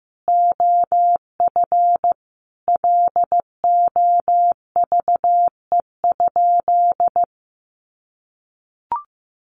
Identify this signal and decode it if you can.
Morse code, heard as a keyed tone, about 15 wpm, 700 Hz. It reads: OF LOVE?
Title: Sentences from Top 1100 Words 15wpm